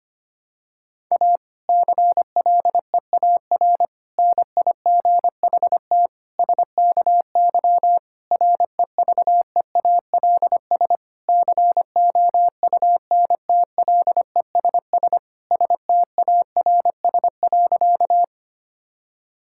Morse code, heard as a keyed tone, about 25 wpm, 700 Hz. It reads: A CLEAR NIGHT SKY REVEALS COUNTLESS STARS.
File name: practice / 25wpm / Difficult Sentences with Hints - 25wpm